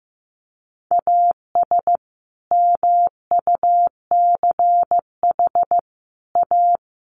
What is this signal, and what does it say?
Morse code, heard as a keyed tone, about 15 wpm, 700 Hz. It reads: AS MUCH A
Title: Sets of 5 Words Encore2 15wpm